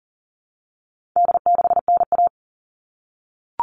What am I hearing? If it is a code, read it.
Morse code, heard as a keyed tone, about 40 wpm, 700 Hz. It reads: D6NA
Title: All Cut Numbers - 4-Digit 40wpm